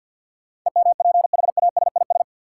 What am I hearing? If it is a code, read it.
Morse code, heard as a keyed tone, about 49 wpm, 700 Hz. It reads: EMPHASIS
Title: Sentences from Top 1400 Words - New Word Drill - 50wpm